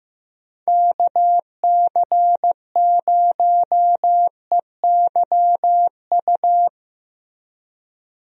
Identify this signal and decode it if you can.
Morse code, heard as a keyed tone, about 15 wpm, 700 Hz. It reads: KC0EYU
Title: USA Call Signs - 6 Characters - Repeated 3x - 15wpm